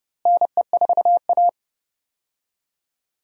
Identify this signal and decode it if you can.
Morse code, heard as a keyed tone, about 30 wpm, 700 Hz. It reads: NE4A
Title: USA Call Signs - 4 Characters - Repeated 3x - 30wpm